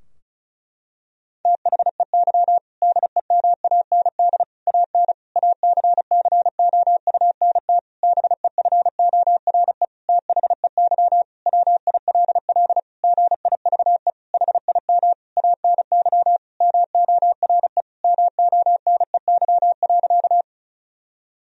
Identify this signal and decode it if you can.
Morse code, heard as a keyed tone, about 34 wpm, 700 Hz. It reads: THEY DEMAND AN ACCOUNT BEFORE THEY WILL GIVE HIM ANY MORE MONEY.
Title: Sentences from Top 1000 Words - Review Part 4 - 35wpm